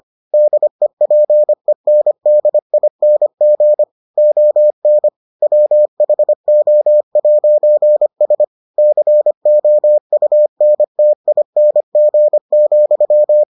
Morse code, heard as a keyed tone, about 25 wpm, 600 Hz. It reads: DEPENDING ON WHO'S COUNTING,